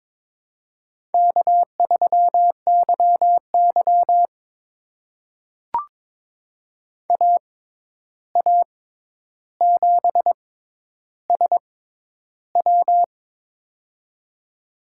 Morse code, heard as a keyed tone, about 23 wpm, spaced as for 11 wpm, 700 Hz. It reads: K3YY AA7SW
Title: USA Call Signs - 6x Character Spacing - 22wpm